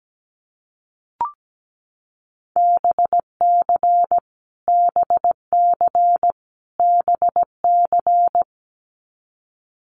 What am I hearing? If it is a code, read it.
Morse code, heard as a keyed tone, about 17 wpm, 700 Hz. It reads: BC BC BC
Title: Canadian Provinces and Territories - Repeated 3x - 17wpm